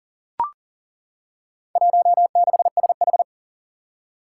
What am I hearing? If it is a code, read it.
Morse code, heard as a keyed tone, about 42 wpm, 700 Hz. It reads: 16SH
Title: Four Challenging Letter-Number Combinations 40wpm